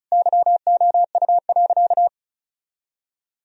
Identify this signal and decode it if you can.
Morse code, heard as a keyed tone, about 34 wpm, 700 Hz. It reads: YOU.
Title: Sentences from Top 1000 Words - Review Part 1 - 35wpm